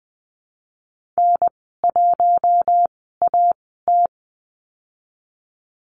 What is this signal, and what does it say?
Morse code, heard as a keyed tone, about 20 wpm, spaced as for 15 wpm, 700 Hz. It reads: N1AT
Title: 042 Lesson 8g - Number 1 - Call Signs 20wpm